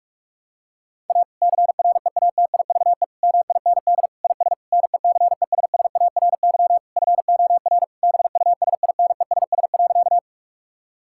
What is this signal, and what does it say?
Morse code, heard as a keyed tone, about 46 wpm, 700 Hz. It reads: A CREATIVE MIND IS NECESSARY FOR BUSINESS.
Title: Sentences from Top 1000 Words - Review Part 5 - 45wpm